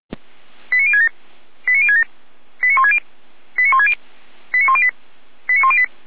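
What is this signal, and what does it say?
Five-tone selective call (ZVEI), heard as quick runs of five tones, tones of about 70 ms. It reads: 89067 89067 89160 89160 89108 89108
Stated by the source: ZVEI five-tone call received on SDR.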